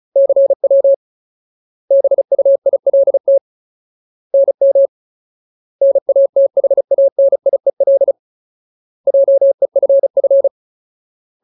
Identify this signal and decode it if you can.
Morse code, heard as a keyed tone, about 34 wpm, 550 Hz. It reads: CW BUILT NM NATHANIEL JEFF
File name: Ultimate Word List / 35WPM Ultimate